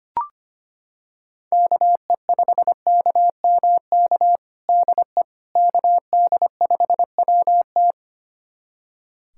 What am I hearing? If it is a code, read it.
Morse code, heard as a keyed tone, about 25 wpm, 700 Hz. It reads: KE5KMK DE KD5WT
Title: USA Callsign DE Callsign 25wpm